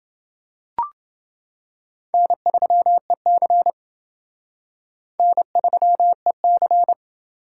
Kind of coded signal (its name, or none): Morse code